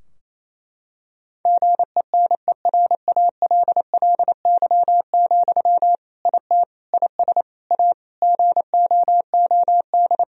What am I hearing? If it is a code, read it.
Morse code, heard as a keyed tone, about 28 wpm, 700 Hz. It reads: GENERALLY, IT IS A GOOD
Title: Sentences from Top 1000 Words - Review Part 4 - 28wpm